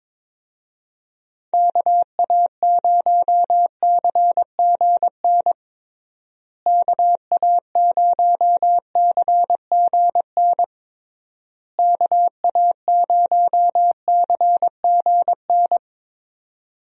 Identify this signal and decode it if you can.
Morse code, heard as a keyed tone, about 22 wpm, 700 Hz. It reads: KA0CGN KA0CGN KA0CGN
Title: USA Call Signs - 6 Characters - Repeated 3x - 22wpm